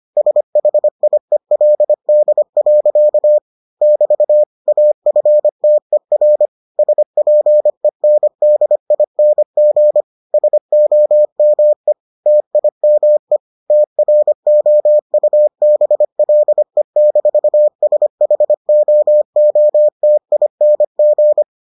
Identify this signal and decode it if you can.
Morse code, heard as a keyed tone, about 25 wpm, 600 Hz. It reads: SHIELD. <BT> AFTER SPENDING SOME TIME TROUBLE-SHOOTING